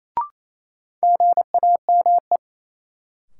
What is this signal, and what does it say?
Morse code, heard as a keyed tone, about 28 wpm, 700 Hz. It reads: GAME
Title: Sentences from Top 500 Words - New Word Drill - 28wpm